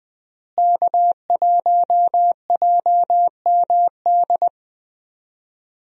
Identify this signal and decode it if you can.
Morse code, heard as a keyed tone, about 20 wpm, 700 Hz. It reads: K1JMD
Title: USA Call Signs - 5 Characters - Repeated 2x - 20wpm